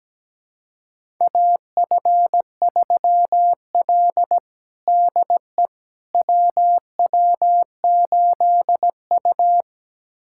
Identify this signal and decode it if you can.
Morse code, heard as a keyed tone, about 17 wpm, 700 Hz. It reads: AF3L DE WW8U
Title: USA Callsign DE Callsign - 4 Characters - 17wpm